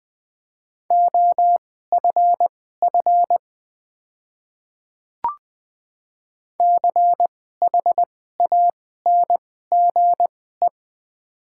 Morse code, heard as a keyed tone, about 20 wpm, spaced as for 14 wpm, 700 Hz. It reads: OFF CHANGE
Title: 2x Character Spacing - Top 300 Words - 20wpm